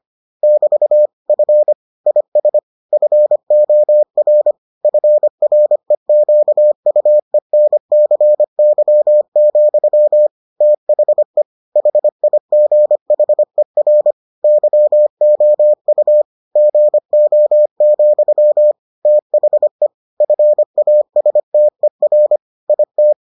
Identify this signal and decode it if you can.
Morse code, heard as a keyed tone, about 25 wpm, 600 Hz. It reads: <BT> F IS FOR FREQUENCY, THE HIGHER YOU GO, THE FASTER IT